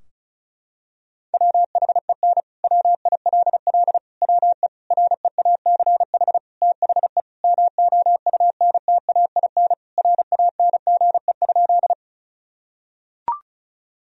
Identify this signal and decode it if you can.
Morse code, heard as a keyed tone, about 36 wpm, 700 Hz. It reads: WHEN WILL WE REACH THE MOUNTAIN RANGE?
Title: Sentences from Top 1000 Words - Review Part 4 - 35wpm